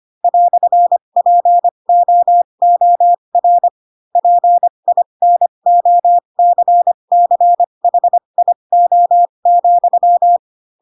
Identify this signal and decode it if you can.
Morse code, heard as a keyed tone, about 25 wpm, spaced as for 22 wpm, 700 Hz. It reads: "POOR PINOCCHIO,
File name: Pinocchio / Pinocchio ch4 18 25 ik1hge